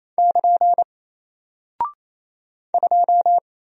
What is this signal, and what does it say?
Morse code, heard as a keyed tone, about 28 wpm, 700 Hz. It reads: <KN> 2
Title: Single Character (Letter:Number:Punctuation:Prosign) Alt 28wpm